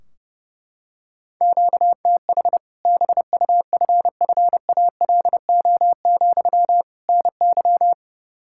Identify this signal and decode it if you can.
Morse code, heard as a keyed tone, about 30 wpm, 700 Hz. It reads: QTH BUFFALO, NY
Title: Practice QSO - Exchange 1 - Beginning - 30wpm